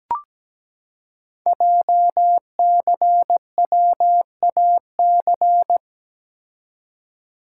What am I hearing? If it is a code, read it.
Morse code, heard as a keyed tone, about 17 wpm, 700 Hz. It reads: JCWAC
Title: Code Groups - 5 Letters - 17wpm